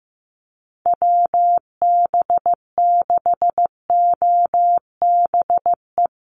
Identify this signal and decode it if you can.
Morse code, heard as a keyed tone, about 15 wpm, 700 Hz. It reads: WB6OBE
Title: USA Call Signs 15wpm